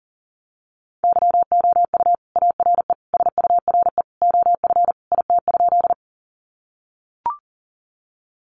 Morse code, heard as a keyed tone, about 40 wpm, 700 Hz. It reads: YOU ARE SURE OF IT?
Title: Sentences from Top 300 Words 40wpm